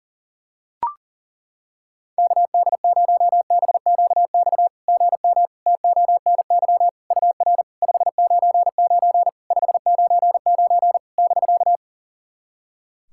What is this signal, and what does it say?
Morse code, heard as a keyed tone, about 40 wpm, 690 Hz. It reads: KD0BQX GM TONY UR 599 599 <BK>
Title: SOTA QSO Practice 40wpm